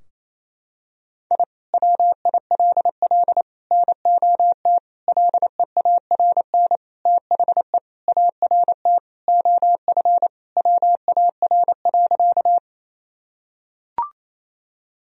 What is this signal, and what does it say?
Morse code, heard as a keyed tone, about 28 wpm, 700 Hz. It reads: I WILL NOT LEARN THE ART OF WAR.
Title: Sentences from Top 1000 Words - Review Part 3 - 28wpm